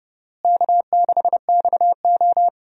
Morse code, heard as a keyed tone, about 32 wpm, 700 Hz. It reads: K6XO
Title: USA Call Signs - 4 Characters - Repeated 3x - 30wpm